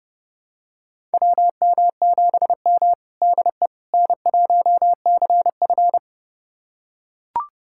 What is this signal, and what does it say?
Morse code, heard as a keyed tone, about 30 wpm, 700 Hz. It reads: WM7M DE N1CF
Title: USA Callsign DE Callsign - 4 Characters - 30wpm